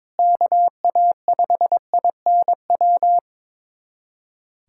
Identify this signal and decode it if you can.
Morse code, heard as a keyed tone, about 22 wpm, 700 Hz. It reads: KA5INW
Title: USA Callsign DE Callsign - 6 Characters - 22wpm